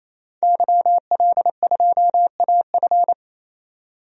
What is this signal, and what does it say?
Morse code, heard as a keyed tone, about 28 wpm, 700 Hz. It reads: YL2AF
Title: DXCC-100 Call Signs - 5 Characters - 28wpm